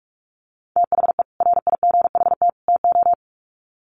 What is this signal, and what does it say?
Morse code, heard as a keyed tone, about 44 wpm, 700 Hz. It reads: THE RIGHT TO
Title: Sets of 3 Words 45wpm